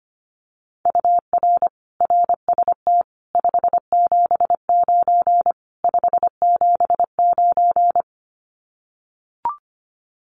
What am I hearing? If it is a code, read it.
Morse code, heard as a keyed tone, about 25 wpm, 700 Hz. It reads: UR RST 579 579
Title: Common QSO Phrases Encore 25wpm